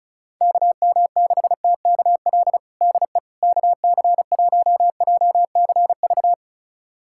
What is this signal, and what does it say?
Morse code, heard as a keyed tone, about 34 wpm, 700 Hz. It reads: KM6TKL DE KC1JCV
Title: USA Callsign DE Callsign 35wpm